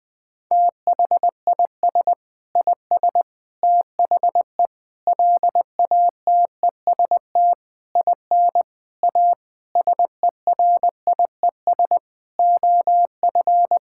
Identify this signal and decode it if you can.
Morse code, heard as a keyed tone, about 20 wpm, 700 Hz. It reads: THIS IS THE LATEST IN A SERIES OF
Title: Sentences with 6-Word Phrases - 20wpm